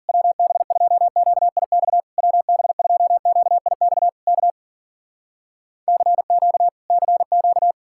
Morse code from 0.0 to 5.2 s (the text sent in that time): WB2QIX WB2QIX K